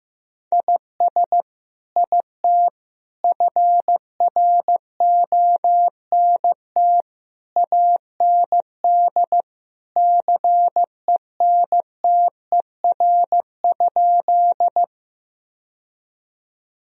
Morse code, heard as a keyed tone, about 15 wpm, 700 Hz. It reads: IS IT FRONT AND CENTER?
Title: Sentences from Top 1000 Words - Review Part 2 - 15wpm